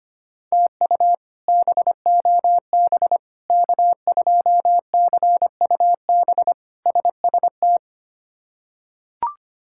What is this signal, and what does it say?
Morse code, heard as a keyed tone, about 25 wpm, 700 Hz. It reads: TU BOB K2CUB SST
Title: K1USN SST QSO Practice - 25wpm